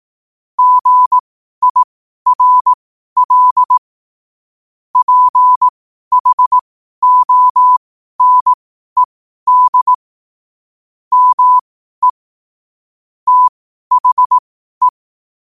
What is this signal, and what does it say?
Morse code, heard as a keyed tone, about 18 wpm, spaced as for 12 wpm, 1000 Hz. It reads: GIRL PHONED ME THE
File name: qotdfiles / 12wpm